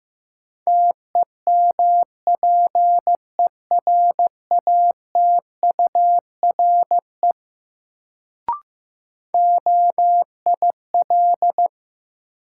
Morse code, heard as a keyed tone, about 15 wpm, 700 Hz. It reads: TEMPERATURE OIL
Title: Sentences from Top 1000 Words - Review Part 4 - New Word Drill - 15wpm